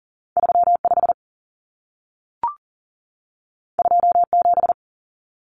Morse code, heard as a keyed tone, about 40 wpm, 700 Hz. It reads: 35 27
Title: Common Cut Numbers - 2-Digit 40wpm